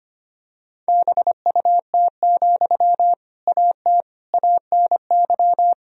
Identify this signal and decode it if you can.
Morse code, heard as a keyed tone, about 25 wpm, 700 Hz. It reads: BUT, AT ANY
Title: Sentences from Top 1000 Words - Review Part 4 - 25wpm